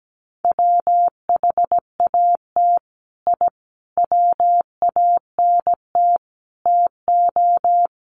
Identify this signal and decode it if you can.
Morse code, heard as a keyed tone, about 17 wpm, 700 Hz. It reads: WHAT I WANT TO
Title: Sets of 5 Words Encore 17wpm